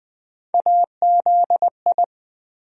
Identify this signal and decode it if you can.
Morse code, heard as a keyed tone, about 20 wpm, 700 Hz. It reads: AZI
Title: USA Call Sign Postfixes - 3 Letters 20wpm